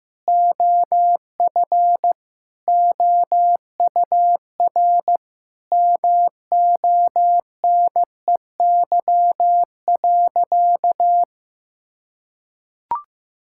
Morse code, heard as a keyed tone, about 15 wpm, 700 Hz. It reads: OF OUR MONEY.
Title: Sentences from Top 1000 Words - Review Part 5 - 15wpm